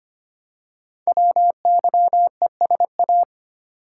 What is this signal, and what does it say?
Morse code, heard as a keyed tone, about 25 wpm, 700 Hz. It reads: WYESA
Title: Code Groups - 5 Letters - 25wpm